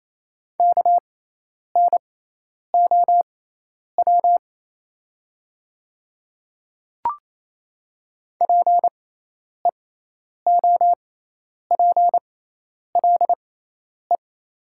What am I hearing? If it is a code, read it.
Morse code, heard as a keyed tone, about 27 wpm, spaced as for 10 wpm, 690 Hz. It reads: KNOW PEOPLE
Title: 6x Character Spacing - Top 100 Words - 28wpm